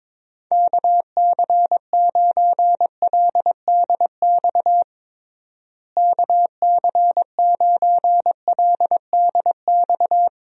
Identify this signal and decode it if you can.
Morse code, heard as a keyed tone, about 22 wpm, 700 Hz. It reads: KC9LDX KC9LDX
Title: USA Call Signs - 6 Characters - Repeated 3x - 22wpm